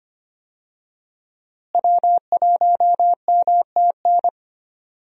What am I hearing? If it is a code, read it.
Morse code, heard as a keyed tone, about 25 wpm, 700 Hz. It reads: W1MTN
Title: USA Call Signs - 5 Characters - Repeated 3x - 25wpm